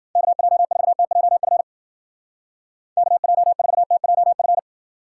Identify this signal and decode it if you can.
Morse code, heard as a keyed tone, about 51 wpm, 700 Hz. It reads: KJ4TJF KJ4TJF